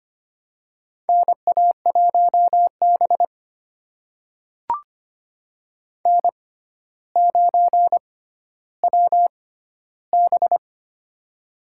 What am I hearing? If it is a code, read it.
Morse code, heard as a keyed tone, about 25 wpm, spaced as for 13 wpm, 700 Hz. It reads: NA1B N9WB
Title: USA Call Signs - 4 Characters - 6x Character Spacing - 25wpm